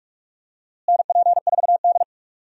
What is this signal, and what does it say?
Morse code, heard as a keyed tone, about 45 wpm, 700 Hz. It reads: NP4D
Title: USA Call Signs - 4 Characters - Repeated 3x - 45wpm